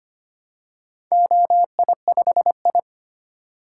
Morse code, heard as a keyed tone, about 25 wpm, 700 Hz. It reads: OI5I
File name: practice / 25wpm / Four Letter-Number Combinations - 25wpm